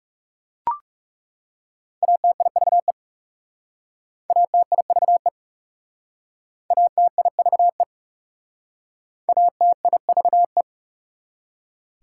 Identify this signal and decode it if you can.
Morse code, heard as a keyed tone, about 39 wpm, 690 Hz. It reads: ATIVE ATIVE ATIVE ATIVE